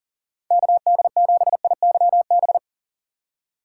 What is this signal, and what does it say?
Morse code, heard as a keyed tone, about 40 wpm, 700 Hz. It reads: KD7IYB
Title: USA Call Signs - 3x Character Spacing - 40wpm